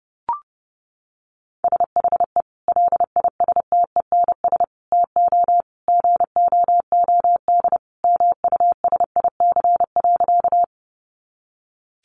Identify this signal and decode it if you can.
Morse code, heard as a keyed tone, about 30 wpm, 700 Hz. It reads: SHE LISTENS TO GOOD MUSIC.
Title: Easy Phrases 30wpm